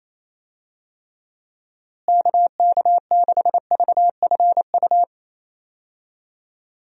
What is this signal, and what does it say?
Morse code, heard as a keyed tone, about 27 wpm, 700 Hz. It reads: KK6VFU
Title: USA Call Signs - 6 Characters - Repeated 3x - 28wpm